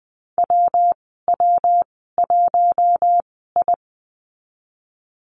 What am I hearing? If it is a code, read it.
Morse code, heard as a keyed tone, about 19 wpm, spaced as for 14 wpm, 700 Hz. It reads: WW1I
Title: 200 Lesson 32g - Letter k - Call Signs 20wpm